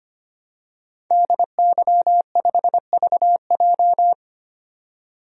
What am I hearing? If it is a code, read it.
Morse code, heard as a keyed tone, about 25 wpm, 700 Hz. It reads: DY5VJ
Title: DXCC-100 Call Signs - 5 Characters - 25wpm